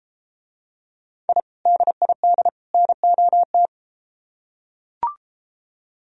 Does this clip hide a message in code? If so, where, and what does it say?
Yes: Morse code from 1.3 to 3.7 s, heard as a keyed tone, about 34 wpm, 700 Hz. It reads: I DID NOT